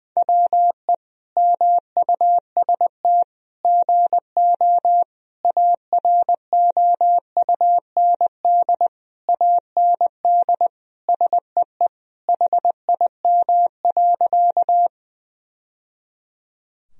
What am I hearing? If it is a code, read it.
Morse code, heard as a keyed tone, about 19 wpm, 700 Hz. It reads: WE MUST GO AROUND AND SEE HIM.
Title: Sentences from Top 1000 Words - Review Part 3 - 20wpm